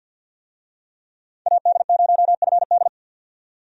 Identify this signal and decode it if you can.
Morse code, heard as a keyed tone, about 50 wpm, 700 Hz. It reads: AD0FD
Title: USA Call Signs - 5 Characters - 50wpm